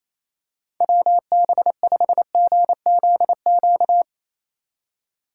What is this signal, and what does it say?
Morse code, heard as a keyed tone, about 28 wpm, 690 Hz. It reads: WB5GZQ
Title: USA Call Signs - 6 Characters - 4x Character Spacing - 28wpm